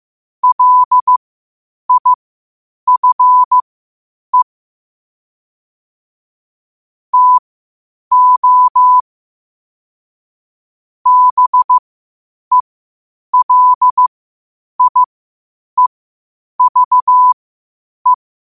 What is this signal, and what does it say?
Morse code, heard as a keyed tone, about 15 wpm, spaced as for 8 wpm, 1000 Hz. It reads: LIFE TO BELIEVE
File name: qotdfiles / 8wpm